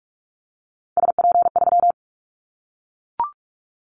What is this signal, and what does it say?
Morse code, heard as a keyed tone, about 44 wpm, 700 Hz. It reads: SP3
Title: Callsign Prefixes for Top 25 DXCC - Repeated 3x - 45wpm